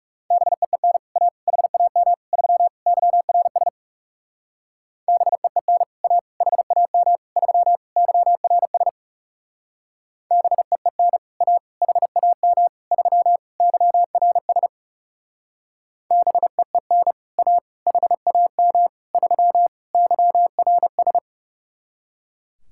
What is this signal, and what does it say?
Morse code, heard as a keyed tone, about 39 wpm, 700 Hz. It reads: BEEN A HAM 3 YRS BEEN A HAM 3 YRS BEEN A HAM 3 YRS BEEN A HAM 3 YRS